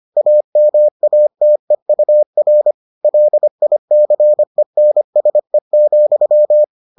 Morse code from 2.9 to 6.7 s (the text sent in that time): LICENSE,